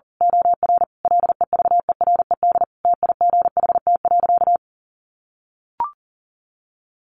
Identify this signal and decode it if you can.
Morse code, heard as a keyed tone, about 40 wpm, 700 Hz. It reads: OR LEVERED TIGHT.
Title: A Sound Of Thunder 40wpm